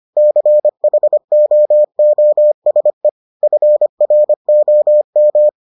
Morse code, heard as a keyed tone, about 25 wpm, 600 Hz. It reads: CHOOSE FROM